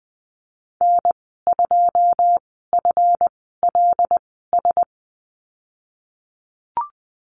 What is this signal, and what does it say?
Morse code, heard as a keyed tone, about 20 wpm, spaced as for 15 wpm, 700 Hz. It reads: N2FLS
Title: 147 Lesson 24g - Letter y - Call Signs 20wpm